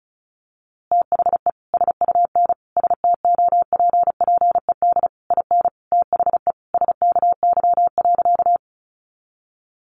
Morse code, heard as a keyed tone, about 34 wpm, 700 Hz. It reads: THE SUN STOPPED IN THE SKY.
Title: A Sound Of Thunder 35wpm